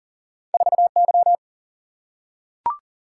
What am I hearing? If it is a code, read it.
Morse code, heard as a keyed tone, about 40 wpm, 700 Hz. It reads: 4Y
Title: Single-Letter Single-Number 40wpm